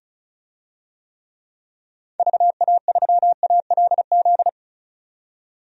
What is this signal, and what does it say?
Morse code, heard as a keyed tone, about 35 wpm, 700 Hz. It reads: VA3ALZ
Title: DXCC-100 Call Signs - 6 Characters - 35wpm